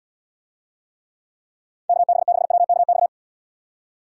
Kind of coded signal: Morse code

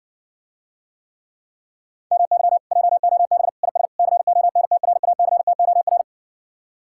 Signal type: Morse code

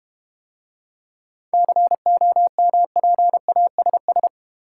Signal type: Morse code

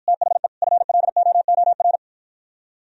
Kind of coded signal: Morse code

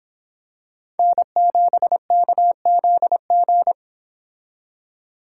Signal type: Morse code

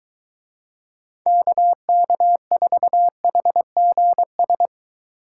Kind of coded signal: Morse code

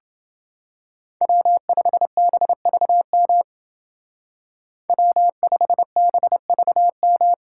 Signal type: Morse code

